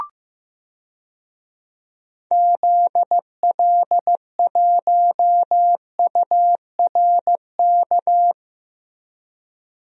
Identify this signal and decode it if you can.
Morse code, heard as a keyed tone, about 15 wpm, 700 Hz. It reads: ZL1URK
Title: DXCC-100 Call Signs - 6 Characters - Repeated 3x - 15wpm